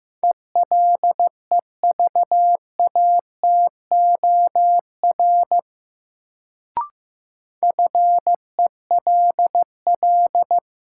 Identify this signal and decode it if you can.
Morse code, heard as a keyed tone, about 15 wpm, 700 Hz. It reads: ELEVATOR FELL
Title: Sentences from Top 1600 Words - New Word Drill Alt - 15wpm